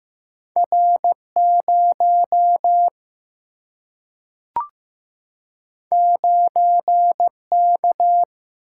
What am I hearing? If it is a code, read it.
Morse code, heard as a keyed tone, about 15 wpm, 700 Hz. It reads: R0 9K
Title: Single-Letter Single-Number 15wpm